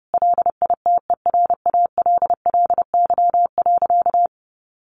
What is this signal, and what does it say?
Morse code, heard as a keyed tone, about 30 wpm, 700 Hz. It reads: LITERALLY.
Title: Sentences from Top 1400 Words 30wpm